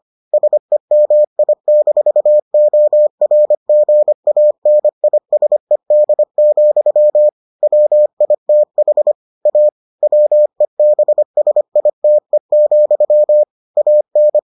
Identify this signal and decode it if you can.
Morse code, heard as a keyed tone, about 25 wpm, 600 Hz. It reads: SEMI-ORGANISED, WITH A WEBSITE, AN